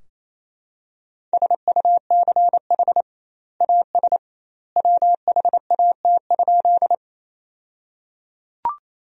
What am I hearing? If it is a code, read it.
Morse code, heard as a keyed tone, about 28 wpm, 700 Hz. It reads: SUCH AS WHAT?
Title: 2x Word-Spacing - Sentences from Top 200 Words - 28wpm